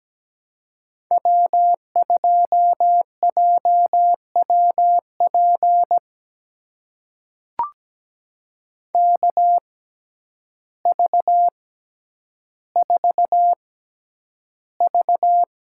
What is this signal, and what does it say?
Morse code, heard as a keyed tone, about 17 wpm, spaced as for 9 wpm, 700 Hz. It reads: W2JWP KV4V
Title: USA Call Signs - 5 Characters - 6x Character Spacing - 17wpm